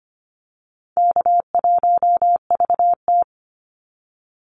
Morse code, heard as a keyed tone, about 25 wpm, 700 Hz. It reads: K1VT
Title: Call Signs - 25wpm